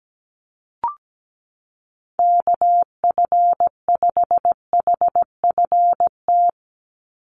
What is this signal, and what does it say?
Morse code, heard as a keyed tone, about 17 wpm, 700 Hz. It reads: KF5HFT
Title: USA Call Signs 17wpm